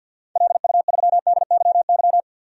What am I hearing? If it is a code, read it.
Morse code, heard as a keyed tone, about 53 wpm, 700 Hz. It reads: LU3DYY
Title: DXCC-100 Call Signs - 6 Characters - Repeated 2x - 50wpm